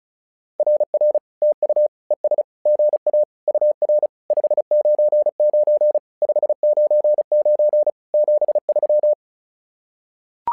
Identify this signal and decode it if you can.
Morse code, heard as a keyed tone, about 36 wpm, 600 Hz. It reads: RR TU ES GA UR 599 599 73
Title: SOTA QSO Practice 35wpm